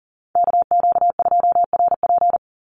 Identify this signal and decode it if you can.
Morse code, heard as a keyed tone, about 40 wpm, 700 Hz. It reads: KQ2RP
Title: NA SOTA Chaser Callsigns 40wpm